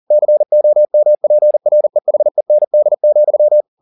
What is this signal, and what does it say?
Morse code, heard as a keyed tone, about 40 wpm, 600 Hz. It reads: COMPREHEND,
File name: the old man and the sea 2 40wpm